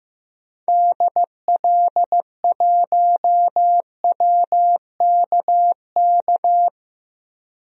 Morse code, heard as a keyed tone, about 16 wpm, 700 Hz. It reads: DL1WKK
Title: DXCC-100 Call Signs - 6 Characters - 15wpm